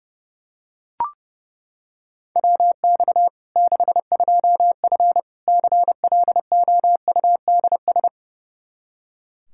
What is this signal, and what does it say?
Morse code, heard as a keyed tone, about 30 wpm, 700 Hz. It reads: WX 62F CLOUDS
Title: K1USN SST QSO Practice 30wpm